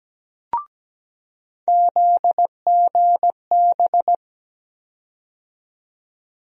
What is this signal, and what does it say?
Morse code, heard as a keyed tone, about 17 wpm, 700 Hz. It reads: ZGB
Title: USA Call Sign Postfixes - 3 Letters 17wpm